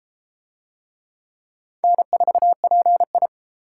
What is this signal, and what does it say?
Morse code, heard as a keyed tone, about 31 wpm, 700 Hz. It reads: N4PI